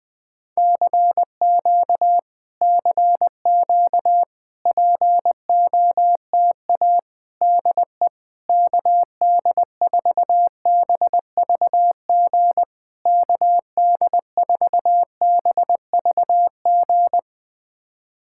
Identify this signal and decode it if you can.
Morse code, heard as a keyed tone, about 20 wpm, 700 Hz. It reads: CQ CQ POTA DE KD4BVG KD4BVG
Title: POTA QSO Practice - 20wpm